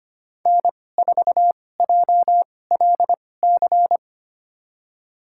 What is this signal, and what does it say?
Morse code, heard as a keyed tone, about 25 wpm, spaced as for 18 wpm, 700 Hz. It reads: N4JLC
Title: USA Call Signs - 5 Characters - 2x Character Spacing - 25wpm